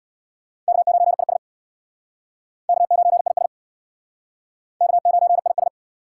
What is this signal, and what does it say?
Morse code, heard as a keyed tone, about 60 wpm, 690 Hz. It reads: K9IS K9IS K9IS
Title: Speed Racing - USA 1x2 Call Signs 50wpm